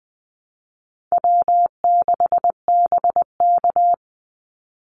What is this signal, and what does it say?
Morse code, heard as a keyed tone, about 20 wpm, 700 Hz. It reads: W6BK
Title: Easy Call Signs - 20wpm